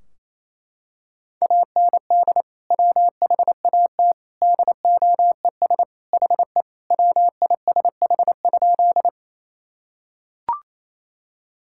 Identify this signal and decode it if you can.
Morse code, heard as a keyed tone, about 28 wpm, 700 Hz. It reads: AND WHAT DOES HE WISH?
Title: Sentences from Top 1600 Words 28wpm